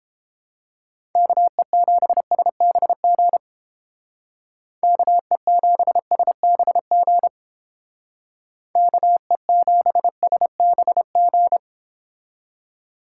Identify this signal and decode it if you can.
Morse code, heard as a keyed tone, about 31 wpm, 700 Hz. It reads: KE7SBG KE7SBG KE7SBG